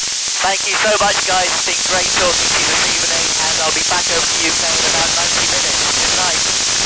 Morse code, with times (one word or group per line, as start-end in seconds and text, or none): none